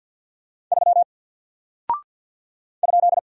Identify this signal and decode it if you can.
Morse code, heard as a keyed tone, about 50 wpm, 700 Hz. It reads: 3 ?
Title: Single Character (Letter:Number:Punctuation:Prosign) Alt 50wpm